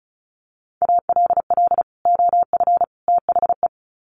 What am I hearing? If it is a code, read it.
Morse code, heard as a keyed tone, about 34 wpm, 700 Hz. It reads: ALL OF THE
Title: Sets of 3 Words 35wpm